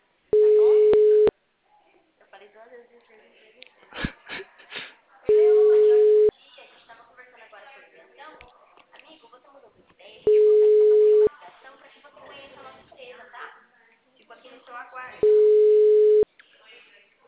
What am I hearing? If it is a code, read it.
Ringback tone (Europe, 425 Hz), with speech.
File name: out- -2003- - - .83569